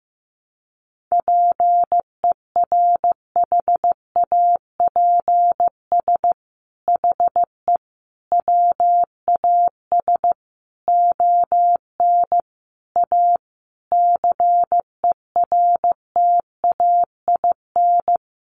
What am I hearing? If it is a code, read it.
Morse code, heard as a keyed tone, about 15 wpm, 700 Hz. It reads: PERHAPS HE WAS ON A CERTAIN
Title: Sentences from Top 1100 Words 15wpm